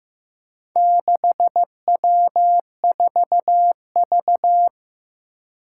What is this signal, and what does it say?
Morse code, heard as a keyed tone, about 15 wpm, 700 Hz. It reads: 6W4V
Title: Four Challenging Letter-Number Combinations 15wpm